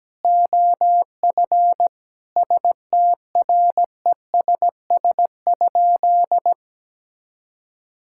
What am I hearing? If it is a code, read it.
Morse code, heard as a keyed tone, about 17 wpm, 700 Hz. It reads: OF STRESS?
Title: Sentences from Top 1000 Words - Review Part 6 - 17wpm